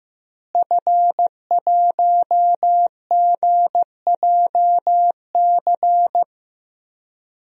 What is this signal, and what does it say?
Morse code, heard as a keyed tone, about 15 wpm, 700 Hz. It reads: F1GJC
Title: DXCC-100 Call Signs - 5 Characters - Repeated 3x - 15wpm